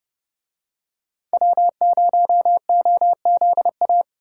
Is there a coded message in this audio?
Yes: Morse code from 1.3 to 4.0 s, heard as a keyed tone, about 29 wpm, 700 Hz. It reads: W0OZA